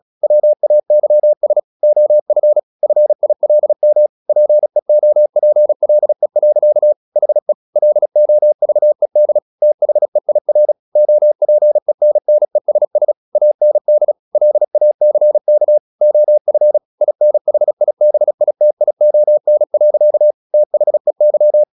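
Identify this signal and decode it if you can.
Morse code, heard as a keyed tone, about 36 wpm, 600 Hz. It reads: WAYS OF FILM PEOPLE. HE LOVED THEIR OPENNESS AND LACK OF INHIBITION. THEY
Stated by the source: Morse practice at 36 wpm, text of Ken Follett's novel Never, from CW med Gnister.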